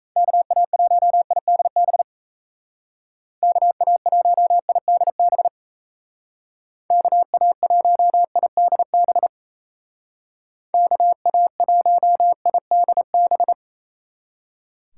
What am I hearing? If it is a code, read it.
Morse code, heard as a keyed tone, about 34 wpm, 700 Hz. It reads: KA1IDB KA1IDB KA1IDB KA1IDB
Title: Speed Racing - USA Call Signs 28wpm